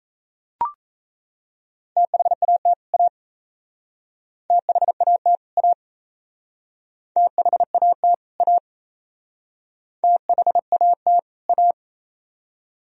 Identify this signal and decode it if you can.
Morse code, heard as a keyed tone, about 36 wpm, 700 Hz. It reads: THAT A THAT A THAT A THAT A